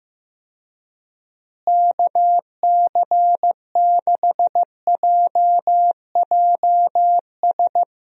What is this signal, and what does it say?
Morse code, heard as a keyed tone, about 15 wpm, 700 Hz. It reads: KC6JJS
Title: USA Call Signs - 6 Characters - Repeated 2x - 15wpm